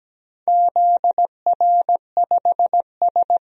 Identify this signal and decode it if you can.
Morse code, heard as a keyed tone, about 17 wpm, 700 Hz. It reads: ZR5S
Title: DXCC-100 Call Signs - 4 Characters - 17wpm